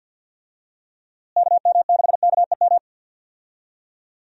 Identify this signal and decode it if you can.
Morse code, heard as a keyed tone, about 53 wpm, 700 Hz. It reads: KM6KEM
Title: USA Call Signs - 6 Characters - Repeated 2x - 50wpm